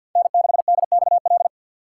Morse code, heard as a keyed tone, about 52 wpm, 700 Hz. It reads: N6DXL
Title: USA Call Signs - 6x Character Spacing - 50wpm